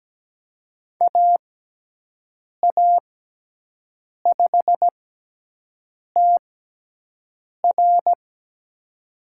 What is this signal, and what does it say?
Morse code, heard as a keyed tone, about 17 wpm, spaced as for 6 wpm, 700 Hz. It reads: AA5TR
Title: USA Call Signs - 6x Character Spacing - 17wpm